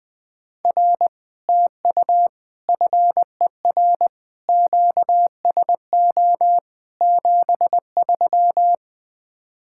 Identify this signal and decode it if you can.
Morse code, heard as a keyed tone, about 20 wpm, 700 Hz. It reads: R TU FER QSO 73
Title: Speed-Racing Common QSO Phrases - 17wpm